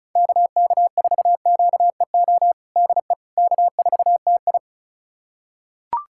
Morse code, heard as a keyed tone, about 36 wpm, 700 Hz. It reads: KK4QEO DE K4TI
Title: USA Callsign DE Callsign 35wpm